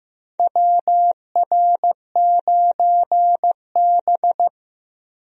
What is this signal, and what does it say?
Morse code, heard as a keyed tone, about 15 wpm, 700 Hz. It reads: WR9B
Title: USA Call Signs - 4 Characters - 15wpm